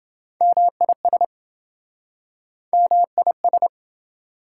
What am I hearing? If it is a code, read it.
Morse code, heard as a keyed tone, about 27 wpm, 700 Hz. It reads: MIS MIS